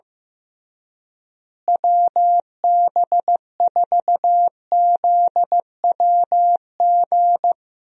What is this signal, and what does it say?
Morse code, heard as a keyed tone, about 15 wpm, 700 Hz. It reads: WB4ZWG
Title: USA Call Signs - 6 Characters - Repeated 3x - 15wpm